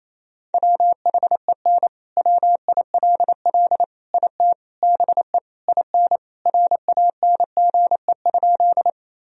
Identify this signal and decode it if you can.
Morse code, heard as a keyed tone, about 27 wpm, 700 Hz. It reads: WHEN WILL IT BE IN RANGE?
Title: Sentences from Top 1000 Words - Review Part 4 - 28wpm